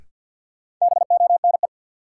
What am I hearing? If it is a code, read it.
Morse code, heard as a keyed tone, about 49 wpm, 700 Hz. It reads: BONE
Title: Sentences from Top 1600 Words - New Word Drill Alt - 50wpm